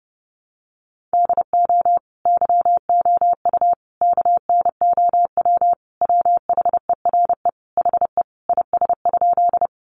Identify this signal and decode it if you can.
Morse code, heard as a keyed tone, about 30 wpm, 700 Hz. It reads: DO YOU KNOW WHERE HE IS?
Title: Sentences from Top 300 Words 30wpm